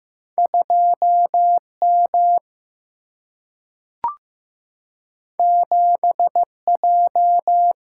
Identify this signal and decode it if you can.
Morse code, heard as a keyed tone, about 15 wpm, 700 Hz. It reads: 2 M 7 J
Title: Single-Letter Single-Number 15wpm